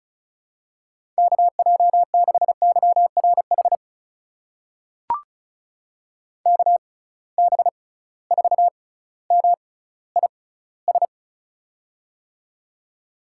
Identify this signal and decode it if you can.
Morse code, heard as a keyed tone, about 36 wpm, spaced as for 18 wpm, 700 Hz. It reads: KJ6YRH KB4MIS
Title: USA Call Signs - 6 Characters - 6x Character Spacing - 35wpm